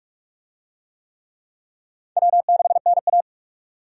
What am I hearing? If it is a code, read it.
Morse code, heard as a keyed tone, about 46 wpm, 700 Hz. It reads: W6NA
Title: USA Call Signs - 4 Characters - 45wpm